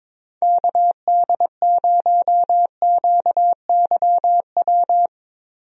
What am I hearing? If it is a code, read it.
Morse code, heard as a keyed tone, about 22 wpm, 700 Hz. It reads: KD0QYW
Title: USA Call Signs - 6 Characters - Repeated 2x - 22wpm